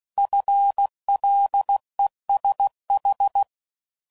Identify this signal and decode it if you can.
Morse code, heard as a keyed tone, about 16 wpm, 800 Hz. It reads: FLESH
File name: Prefixes / FLs & THs / FL Runs